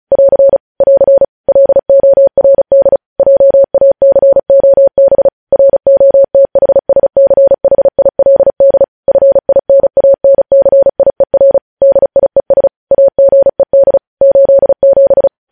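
Morse code, heard as a keyed tone, about 36 wpm, 550 Hz. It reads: <AR> <AR> LORD JACOB ROTHSCHILD FINANCIER DIES AGED 87
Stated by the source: Morse code practice reading news headlines at 35 words per minute.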